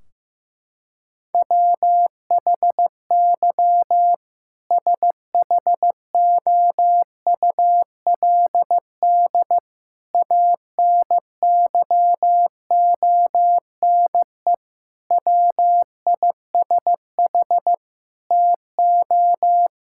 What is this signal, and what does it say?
Morse code, heard as a keyed tone, about 15 wpm, 700 Hz. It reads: WHY SHOULD ANYONE WISH TO
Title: Sentences from Top 1600 Words 15wpm